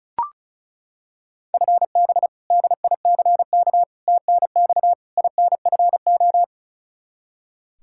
Morse code, heard as a keyed tone, about 36 wpm, 700 Hz. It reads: FB DICK TNX INFO
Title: Practice QSO - Exchange 3 - More Info - 35wpm